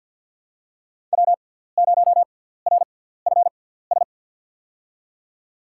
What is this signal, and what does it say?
Morse code, heard as a keyed tone, about 53 wpm, spaced as for 17 wpm, 700 Hz. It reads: W0RFS
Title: USA Call Signs - 5 Characters - 6x Character Spacing - 50wpm